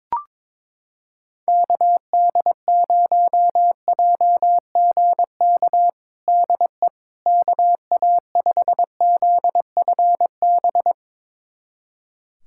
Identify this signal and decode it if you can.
Morse code, heard as a keyed tone, about 22 wpm, 700 Hz. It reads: KD0JGK DE KA5ZFB
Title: USA Callsign DE Callsign - 6 Characters - 22wpm